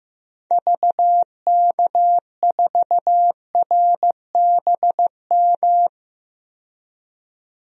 Morse code, heard as a keyed tone, about 15 wpm, 700 Hz. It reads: VK4RBM
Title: DXCC-100 Call Signs - 6 Characters - 15wpm